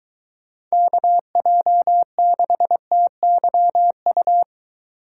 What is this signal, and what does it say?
Morse code, heard as a keyed tone, about 23 wpm, 700 Hz. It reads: KJ6TYU